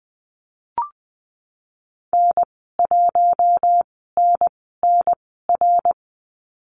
Morse code, heard as a keyed tone, about 20 wpm, 700 Hz. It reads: N 1 N N R
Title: 076 Lesson 13g - Letter L - Call Signs 20wpm